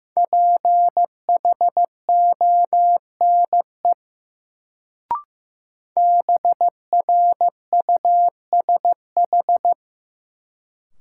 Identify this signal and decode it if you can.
Morse code, heard as a keyed tone, about 15 wpm, 700 Hz. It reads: PHONE BRUSH
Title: Sentences from Top 1100 Words - New Word Drill Alt - 15wpm